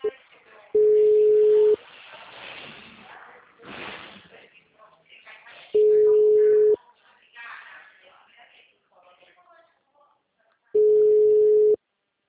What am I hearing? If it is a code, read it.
Ringback tone (Europe, 425 Hz), with speech.